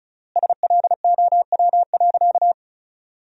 Morse code, heard as a keyed tone, about 34 wpm, 700 Hz. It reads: SLOW.
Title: 3x Word-Spacing - Sentences from Top 400 Words - 35wpm